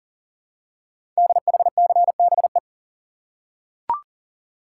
Morse code, heard as a keyed tone, about 42 wpm, 690 Hz. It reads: DHCBE
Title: Code Groups - 5 Letters - 40wpm